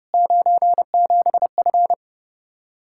Morse code, heard as a keyed tone, about 31 wpm, 700 Hz. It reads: 97F